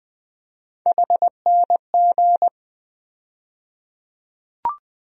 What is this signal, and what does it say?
Morse code, heard as a keyed tone, about 20 wpm, 700 Hz. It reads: HNG
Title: USA Call Sign Postfixes - 3 Letters 20wpm